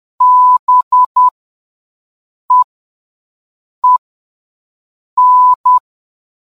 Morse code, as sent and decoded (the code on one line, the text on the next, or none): -... . -.
BEEN